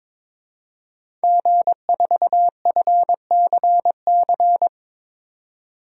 Morse code, heard as a keyed tone, about 22 wpm, 700 Hz. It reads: G4FCC
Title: DXCC-100 Call Signs - 5 Characters - 22wpm